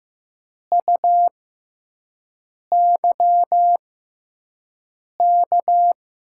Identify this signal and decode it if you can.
Morse code, heard as a keyed tone, about 15 wpm, spaced as for 5 wpm, 700 Hz. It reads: UYK
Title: USA Call Signs - 6 Characters - 6x Character Spacing - 15wpm